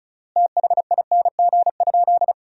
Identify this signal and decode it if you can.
Morse code, heard as a keyed tone, about 34 wpm, 700 Hz. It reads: THING?
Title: 3x Word-Spacing - Sentences from Top 200 Words - 35wpm